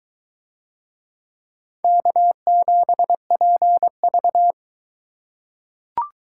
Morse code, heard as a keyed tone, about 22 wpm, 700 Hz. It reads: K7PV